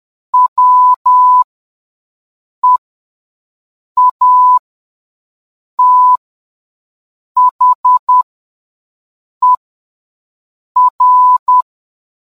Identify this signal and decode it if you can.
Morse code, heard as a keyed tone, about 10 wpm, spaced as for 5 wpm, 1000 Hz. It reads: WEATHER